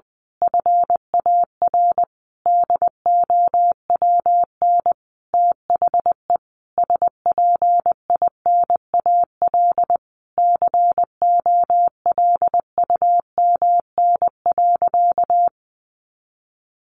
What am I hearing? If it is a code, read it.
Morse code, heard as a keyed tone, about 20 wpm, 700 Hz. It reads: FAR DOWN THE SPINAL COLUMN.
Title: A Sound Of Thunder 20wpm